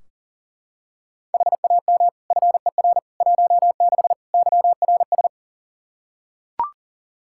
Morse code, heard as a keyed tone, about 40 wpm, 700 Hz. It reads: HAM FER 16 YRS
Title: Practice QSO - Exchange 3 - More Info - 40wpm